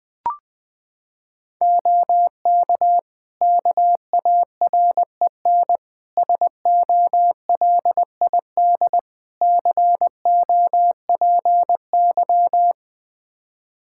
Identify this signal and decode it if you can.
Morse code, heard as a keyed tone, about 20 wpm, 700 Hz. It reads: OK KAREN SOLID COPY
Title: Practice QSO - Exchange 3 - More Info - 20wpm